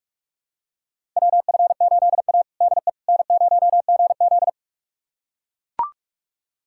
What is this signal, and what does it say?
Morse code, heard as a keyed tone, about 41 wpm, 700 Hz. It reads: WF8A DE N0GZ
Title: USA Callsign DE Callsign - 4 Characters - 45wpm